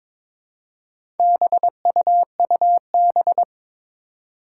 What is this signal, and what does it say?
Morse code, heard as a keyed tone, about 22 wpm, 700 Hz. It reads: BUUB
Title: Four Challenging Letter-Number Combinations 22wpm